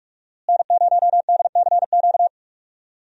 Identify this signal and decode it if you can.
Morse code, heard as a keyed tone, about 46 wpm, 700 Hz. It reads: N0DCQ
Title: USA Call Signs - 5 Characters - Repeated 3x - 45wpm